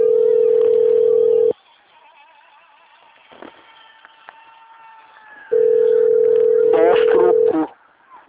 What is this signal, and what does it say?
Ringback tone (North America), 440 + 480 Hz, with speech.
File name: rg-1600-unknown- - - .32338